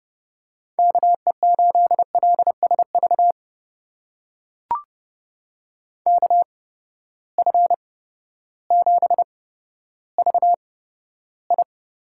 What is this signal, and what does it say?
Morse code, heard as a keyed tone, about 30 wpm, spaced as for 14 wpm, 700 Hz. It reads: KE8LSV KF7VI
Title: USA Call Signs - 6 Characters - 8x Character Spacing - 30wpm